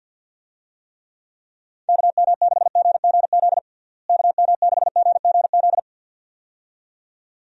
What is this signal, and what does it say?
Morse code, heard as a keyed tone, about 46 wpm, 700 Hz. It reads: KM6GGZ KM6GGZ
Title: USA Call Signs - 6 Characters - Repeated 2x - 50wpm